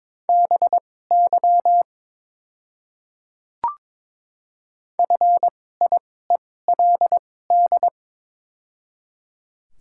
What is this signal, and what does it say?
Morse code, heard as a keyed tone, about 22 wpm, spaced as for 16 wpm, 700 Hz. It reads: BY FIELD
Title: 2x Character Spacing - Top 500 Words - 22wpm